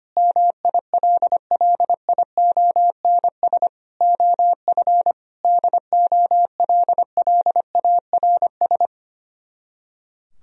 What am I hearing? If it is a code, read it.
Morse code, heard as a keyed tone, about 26 wpm, 700 Hz. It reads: MILLIONS OF DOLLARS
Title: Speed-Racing - Sets of 5 Words 25wpm